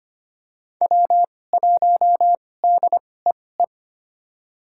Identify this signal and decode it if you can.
Morse code, heard as a keyed tone, about 25 wpm, spaced as for 18 wpm, 700 Hz. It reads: W1DEE
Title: USA Call Signs - 5 Characters - 2x Character Spacing - 25wpm